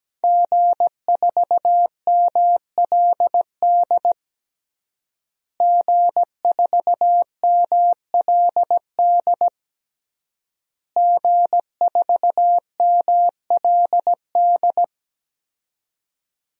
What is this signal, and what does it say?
Morse code, heard as a keyed tone, about 17 wpm, 700 Hz. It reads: G4MLD G4MLD G4MLD
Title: DXCC-100 Call Signs - 5 Characters - Repeated 3x - 17wpm